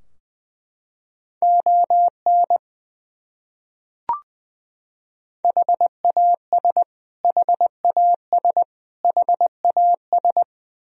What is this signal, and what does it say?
Morse code, heard as a keyed tone, about 20 wpm, 700 Hz. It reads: ON HAS HAS HAS
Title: Top 100 Words Repeated 20wpm